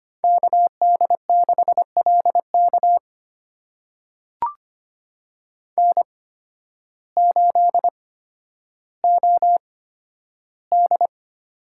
Morse code, heard as a keyed tone, about 25 wpm, spaced as for 12 wpm, 700 Hz. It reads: KD6LK N8OD
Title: USA Call Signs - 5 Characters - 8x Character Spacing - 25wpm